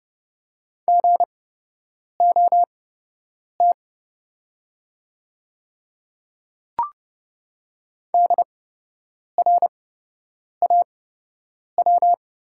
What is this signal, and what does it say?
Morse code, heard as a keyed tone, about 32 wpm, spaced as for 8 wpm, 700 Hz. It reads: GOT DRAW
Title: 8x Character Spacing - Top 300 Words - 30wpm